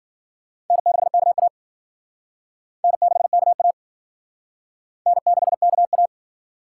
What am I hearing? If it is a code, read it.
Morse code, heard as a keyed tone, about 50 wpm, 700 Hz. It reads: N6KA N6KA N6KA